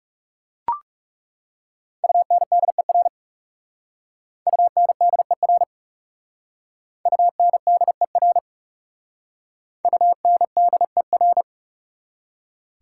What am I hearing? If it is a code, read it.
Morse code, heard as a keyed tone, about 38 wpm, 690 Hz. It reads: UNDER UNDER UNDER UNDER